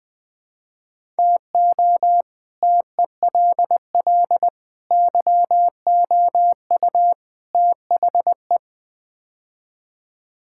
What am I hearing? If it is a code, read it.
Morse code, heard as a keyed tone, about 20 wpm, 700 Hz. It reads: TO TELL YOU THE
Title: Build A Phrase - 5-Word Phrases - 20wpm